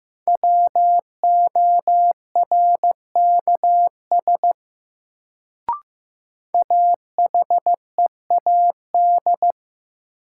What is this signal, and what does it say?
Morse code, heard as a keyed tone, about 15 wpm, 700 Hz. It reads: WORKS AHEAD
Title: Sentences from Top 1100 Words - New Word Drill Alt - 15wpm